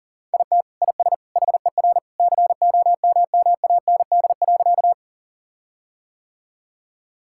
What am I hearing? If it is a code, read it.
Morse code, heard as a keyed tone, about 40 wpm, 700 Hz. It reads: IT IS HER COMMAND.
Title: Sentences from Top 1600 Words 40wpm